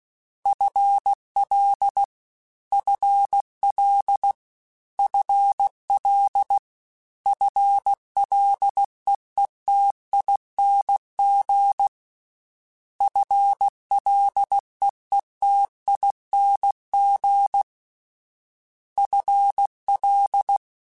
Morse code, heard as a keyed tone, about 16 wpm, 800 Hz. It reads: FL FL FL FLEETING FLEETING FL
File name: FL / FL mp3s / FL Prefixes 3x2 Twice @15